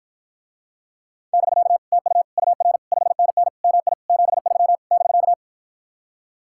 Morse code, heard as a keyed tone, about 53 wpm, 700 Hz. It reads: <BK> TU UR 5NN MI 73 <BK>